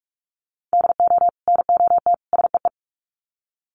Morse code, heard as a keyed tone, about 44 wpm, 700 Hz. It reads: DO NOT SEE
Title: Sets of 3 Words 45wpm